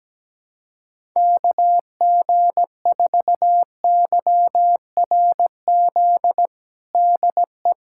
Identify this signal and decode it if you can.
Morse code, heard as a keyed tone, about 17 wpm, 700 Hz. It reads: KG4YRZ DE
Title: USA Callsign DE Callsign - 6 Characters - 17wpm